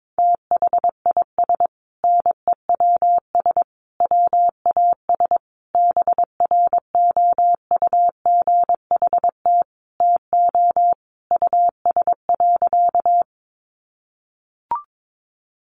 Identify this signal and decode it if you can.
Morse code, heard as a keyed tone, about 22 wpm, 700 Hz. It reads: THIS NEWS WAS BROUGHT TO US.
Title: Sentences from Top 800 Words 22wpm